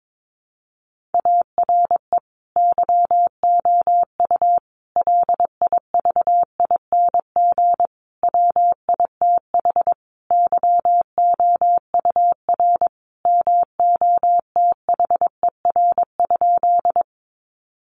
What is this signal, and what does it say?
Morse code, heard as a keyed tone, about 22 wpm, 700 Hz. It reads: ARE YOU LIVING WITH YOUR MOTHER?
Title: Sentences from Top 1100 Words 22wpm